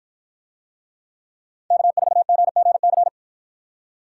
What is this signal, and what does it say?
Morse code, heard as a keyed tone, about 50 wpm, 700 Hz. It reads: K4GGC